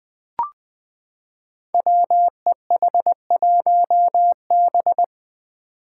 Morse code, heard as a keyed tone, about 20 wpm, 700 Hz. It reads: WEH1B
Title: Five Challenging Letter-Number Combinations 20wpm